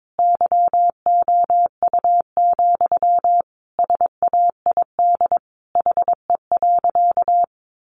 Morse code, heard as a keyed tone, about 22 wpm, 700 Hz. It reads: YOU, SAID HE.
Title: Sentences from Top 400 Words 22wpm